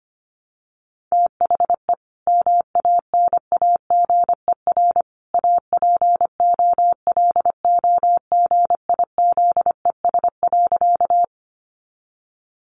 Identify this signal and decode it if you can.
Morse code, heard as a keyed tone, about 25 wpm, 700 Hz. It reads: THE MANAGER APOLOGIZES.
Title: Additional Easy Phrases 25wpm